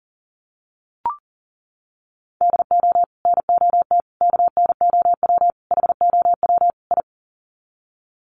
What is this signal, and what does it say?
Morse code, heard as a keyed tone, about 40 wpm, 700 Hz. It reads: DO NOT KNOW HOW I
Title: Sets of 5 Words Encore2 40wpm